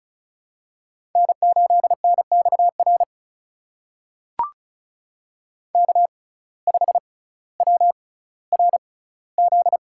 Morse code, heard as a keyed tone, about 36 wpm, spaced as for 18 wpm, 700 Hz. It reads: N8NXR K5WRZ
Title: USA Call Signs - 5 Characters - 6x Character Spacing - 35wpm